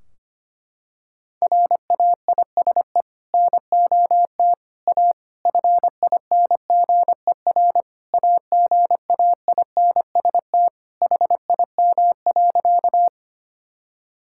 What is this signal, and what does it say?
Morse code, heard as a keyed tone, about 25 wpm, 700 Hz. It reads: RAISE NOT A FINGER AGAINST HIM.
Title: Sentences from Top 1600 Words 25wpm